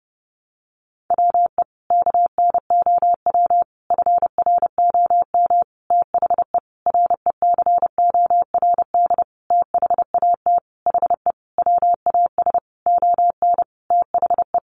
Morse code, heard as a keyed tone, about 30 wpm, 700 Hz. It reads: WE KNOW FROM THE RECORD THAT HE WAS ON THE
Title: Sentences from Top 500 Words 30wpm